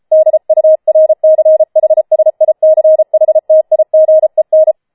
Morse code, heard as a keyed tone, about 33 wpm, 630 Hz. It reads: DURCHSICHTIGEN